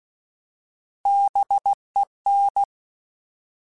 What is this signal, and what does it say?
Morse code, heard as a keyed tone, about 16 wpm, 800 Hz. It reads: BEN
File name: Alphabetical / Alphabetical mp3s / Bs @15